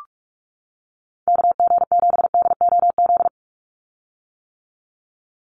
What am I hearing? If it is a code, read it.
Morse code, heard as a keyed tone, about 44 wpm, 700 Hz. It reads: KG7DOZ
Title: USA Call Signs 45wpm